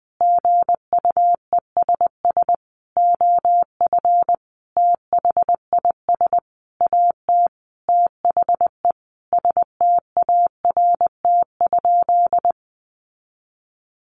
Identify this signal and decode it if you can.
Morse code, heard as a keyed tone, about 20 wpm, 700 Hz. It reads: GUESS OF THIS AT THE START?
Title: Sentences from Top 1500 Words 20wpm